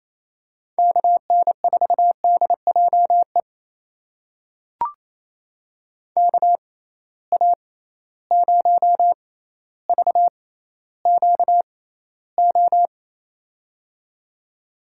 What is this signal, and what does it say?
Morse code, heard as a keyed tone, about 28 wpm, spaced as for 14 wpm, 700 Hz. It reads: KN4DJE KA0VQO
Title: USA Call Signs - 6 Characters - 6x Character Spacing - 28wpm